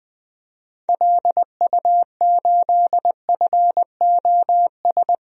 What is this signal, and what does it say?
Morse code, heard as a keyed tone, about 20 wpm, 700 Hz. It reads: LU8FOS
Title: DXCC-100 Call Signs - 6 Characters - Repeated 3x - 20wpm